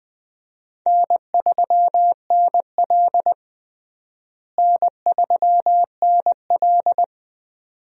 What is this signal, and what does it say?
Morse code, heard as a keyed tone, about 20 wpm, 700 Hz. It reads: N3NL N3NL
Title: USA Call Signs - 4 Characters - Repeated 2x - 20wpm